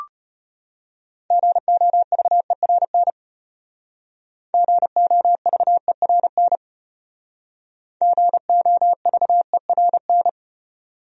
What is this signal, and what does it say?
Morse code, heard as a keyed tone, about 32 wpm, 700 Hz. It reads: GOVERN GOVERN GOVERN